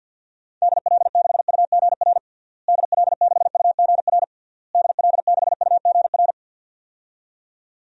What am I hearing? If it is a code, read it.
Morse code, heard as a keyed tone, about 52 wpm, 700 Hz. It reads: DL6UGR DL6UGR DL6UGR
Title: DXCC-100 Call Signs - 6 Characters - Repeated 3x - 50wpm